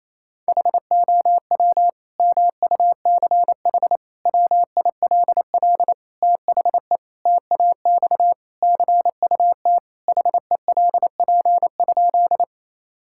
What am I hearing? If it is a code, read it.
Morse code, heard as a keyed tone, about 28 wpm, 700 Hz. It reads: HOW MUCH WILL THE TAX CUT HELP?
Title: Sentences from Top 1000 Words - Review Part 5 - 28wpm